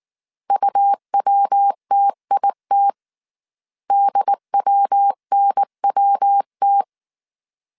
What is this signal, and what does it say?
Morse code, heard as a keyed tone, about 19 wpm, 800 Hz. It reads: UWTIT DWNWT